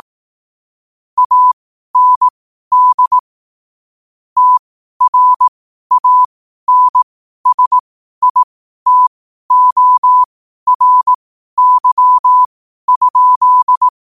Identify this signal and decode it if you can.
Morse code, heard as a keyed tone, about 18 wpm, spaced as for 12 wpm, 1000 Hz. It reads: AND TRANSITORY?
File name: qotdfiles / 12wpm